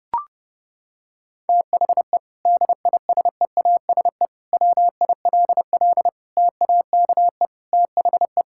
Morse code, heard as a keyed tone, about 30 wpm, 700 Hz. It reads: THE DISEASE WILL TAKE THE
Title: Sentences from Top 1000 Words - Review Part 6 - 30wpm